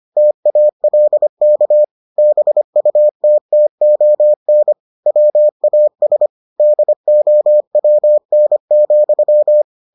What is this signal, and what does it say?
Morse code, heard as a keyed tone, about 25 wpm, 600 Hz. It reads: TALK BUTTON WAS DOWN,